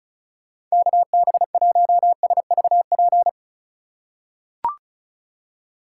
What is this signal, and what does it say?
Morse code, heard as a keyed tone, about 36 wpm, 700 Hz. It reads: KB1SVP
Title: USA Call Signs - 6 Characters - 2x Character Spacing - 35wpm